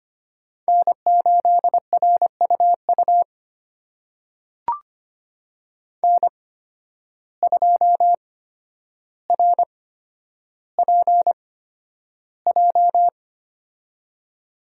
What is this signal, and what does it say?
Morse code, heard as a keyed tone, about 25 wpm, spaced as for 11 wpm, 700 Hz. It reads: N8RUU N2RPJ
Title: USA Call Signs - 5 Characters - 8x Character Spacing - 25wpm